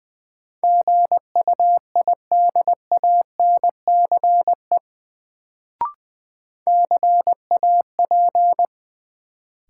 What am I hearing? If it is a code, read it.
Morse code, heard as a keyed tone, about 21 wpm, 700 Hz. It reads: GUIDANCE CAP
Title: Sentences from Top 1600 Words - New Word Drill Alt - 20wpm